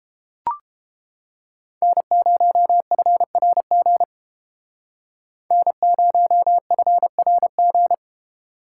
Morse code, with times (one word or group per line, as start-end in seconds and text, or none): 1.8-4.0 N0FRG
5.5-7.9 N0FRG